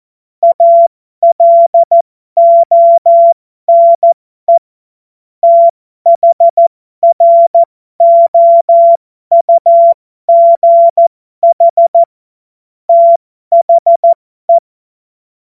Morse code, heard as a keyed tone, about 14 wpm, spaced as for 12 wpm, 680 Hz. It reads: ALONE THROUGH THE
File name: Morsekurs / Witze